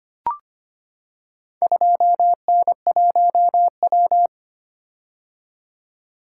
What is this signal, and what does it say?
Morse code, heard as a keyed tone, about 25 wpm, 700 Hz. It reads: 2N1W
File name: practice / 25wpm / Four Letter-Number Combinations - 25wpm